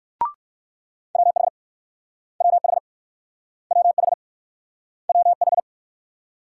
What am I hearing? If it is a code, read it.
Morse code, heard as a keyed tone, about 55 wpm, 700 Hz. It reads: WH WH WH WH
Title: Speed-Racing Multi-letter Phonemes - 45wpm